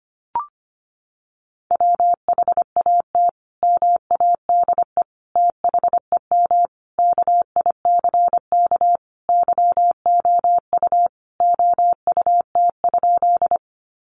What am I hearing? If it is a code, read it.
Morse code, heard as a keyed tone, about 25 wpm, 700 Hz. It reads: WHAT MADE THEM KICK YOU OUT?
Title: Sentences from Top 1300 Words 25wpm